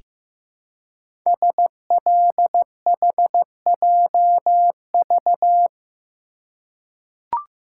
Morse code, heard as a keyed tone, about 15 wpm, 700 Hz. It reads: SLHJV
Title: Code Groups - 5 Letters - 15wpm